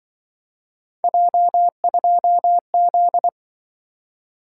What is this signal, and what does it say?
Morse code, heard as a keyed tone, about 24 wpm, 700 Hz. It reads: J2Z
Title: Speed-Racing - Three Letter-Number Combinations - 20wpm